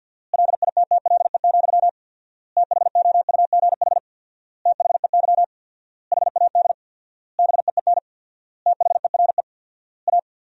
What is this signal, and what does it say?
Morse code, heard as a keyed tone, about 51 wpm, 700 Hz. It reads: LITTLE, THOUGH THEY HAD BEEN THERE A
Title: 4x Word-Spacing - Sentences from Top 400 Words - 50wpm